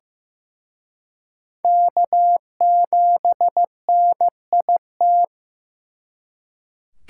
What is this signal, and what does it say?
Morse code, heard as a keyed tone, about 18 wpm, 700 Hz. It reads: K7NIT
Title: Speed Racing - USA 1x3 Call Signs 15wpm